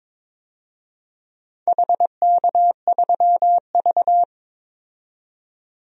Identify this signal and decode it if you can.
Morse code, heard as a keyed tone, about 22 wpm, 700 Hz. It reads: HK3V
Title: DXCC-100 Call Signs - 4 Characters - 22wpm